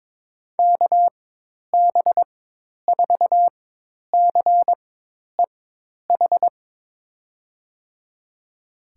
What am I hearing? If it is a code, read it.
Morse code, heard as a keyed tone, about 22 wpm, spaced as for 10 wpm, 700 Hz. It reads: KB4CEH
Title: USA Call Signs - 6 Characters - 4x Character Spacing - 22wpm